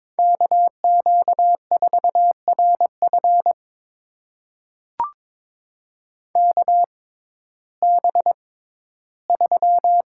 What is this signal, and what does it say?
Morse code, heard as a keyed tone, about 22 wpm, spaced as for 13 wpm, 700 Hz. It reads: KQ4RF KB3
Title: USA Call Signs - 5 Characters - 6x Character Spacing - 22wpm